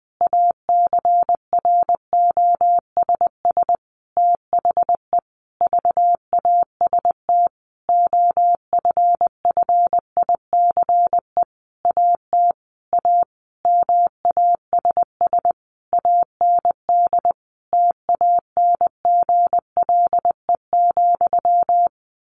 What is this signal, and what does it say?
Morse code, heard as a keyed tone, about 20 wpm, 700 Hz. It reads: ACROSS THE VAST OFFICE AT A MASS AND TANGLE,
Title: A Sound Of Thunder 20wpm